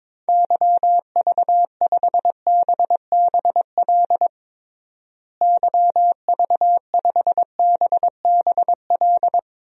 Morse code, heard as a keyed tone, about 22 wpm, 700 Hz. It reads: YV5BBL YV5BBL
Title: DXCC-100 Call Signs - 6 Characters - Repeated 3x - 22wpm